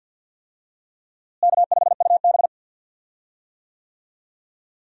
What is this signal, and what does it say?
Morse code, heard as a keyed tone, about 52 wpm, 700 Hz. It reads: K5UB
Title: USA Call Signs - 4 Characters - Repeated 2x - 50wpm